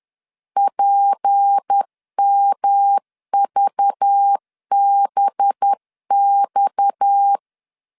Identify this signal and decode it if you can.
Morse code, heard as a keyed tone, about 10 wpm, 800 Hz. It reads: PMVBX
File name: LEZIONI / 7 GF9Z / 7 G 10wpm 2501